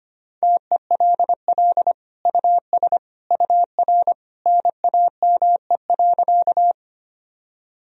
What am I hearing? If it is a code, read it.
Morse code, heard as a keyed tone, about 25 wpm, 700 Hz. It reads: TELL US UR NAME.
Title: Sentences from Top 200 Words and Abbreviations 25wpm